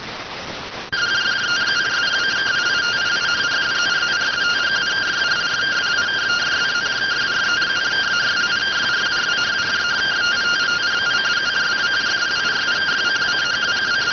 Two-tone teleprinter signal, about 170 Hz shift.